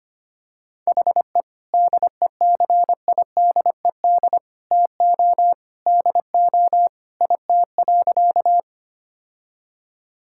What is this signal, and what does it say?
Morse code, heard as a keyed tone, about 25 wpm, 700 Hz. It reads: HE DECIDED TO DO IT.
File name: practice / 25wpm / Sentences with 3-Word Phrases - 25wpm